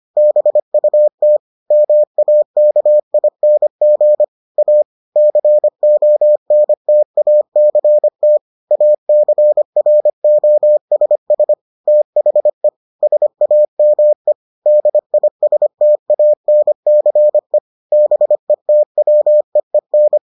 Morse code, heard as a keyed tone, about 25 wpm, 600 Hz. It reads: BUT MAKING A CONTACT ACROSS THE SAME DISTANCE BETWEEN